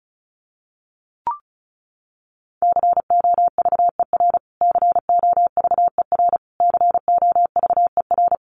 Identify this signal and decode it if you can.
Morse code, heard as a keyed tone, about 34 wpm, 700 Hz. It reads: COVER COVER COVER
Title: Top 300 Words Repeated 35wpm